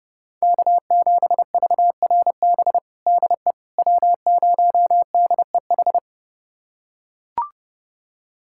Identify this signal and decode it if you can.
Morse code, heard as a keyed tone, about 32 wpm, 700 Hz. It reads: K7VRB DE W0DEH
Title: USA Callsign DE Callsign - 5 Characters - 30wpm